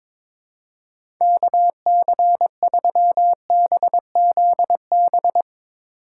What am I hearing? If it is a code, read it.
Morse code, heard as a keyed tone, about 22 wpm, 700 Hz. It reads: KC3BZB
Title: USA Call Signs - 6 Characters - 2x Character Spacing - 22wpm